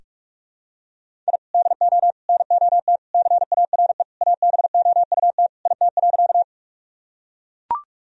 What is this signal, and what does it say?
Morse code, heard as a keyed tone, about 41 wpm, 700 Hz. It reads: I DO NOT CARE ABOUT IT.
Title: Sentences from Top 1000 Words - Review Part 1 - 45wpm